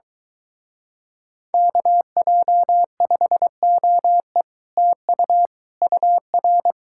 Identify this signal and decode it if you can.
Morse code, heard as a keyed tone, about 24 wpm, 700 Hz. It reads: KJ5OE TU UR